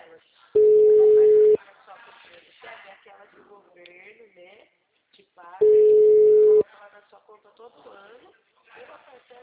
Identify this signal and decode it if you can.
Ringback tone (Europe, 425 Hz), with speech.